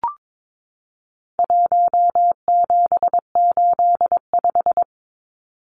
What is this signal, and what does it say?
Morse code, heard as a keyed tone, about 22 wpm, 700 Hz. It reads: 1785
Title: Numbers - 4-Digit 22wpm